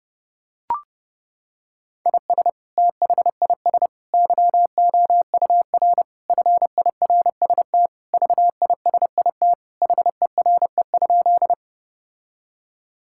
Morse code, heard as a keyed tone, about 30 wpm, 700 Hz. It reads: IS THIS YOUR FIRST VISIT HERE?
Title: Sentences from Top 1000 Words - Review Part 6 - 30wpm